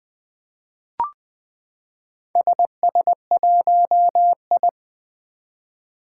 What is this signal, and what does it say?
Morse code, heard as a keyed tone, about 20 wpm, 700 Hz. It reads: SS1I
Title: Four Challenging Letter-Number Combinations 20wpm